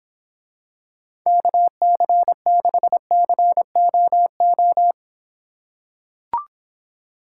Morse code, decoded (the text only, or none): KC6COO